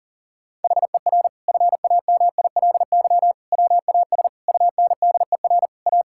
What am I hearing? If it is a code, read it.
Morse code, heard as a keyed tone, about 40 wpm, 700 Hz. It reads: HER FAMILY WAS UNDER A
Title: Sentences from Top 1000 Words - Review Part 6 - 40wpm